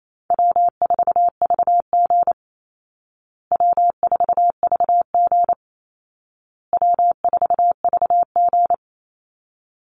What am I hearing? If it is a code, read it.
Morse code, heard as a keyed tone, about 28 wpm, 700 Hz. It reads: W4VG W4VG W4VG
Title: Easy Call Signs Repeated 3x - 28wpm